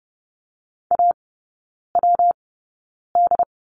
Morse code, heard as a keyed tone, about 30 wpm, spaced as for 9 wpm, 700 Hz. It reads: AWD
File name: Lesson 19e - Letter w - Three Characters 30wpm